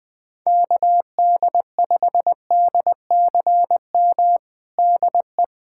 Morse code, heard as a keyed tone, about 20 wpm, 700 Hz. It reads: KD5DCM DE
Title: USA Callsign DE Callsign 20wpm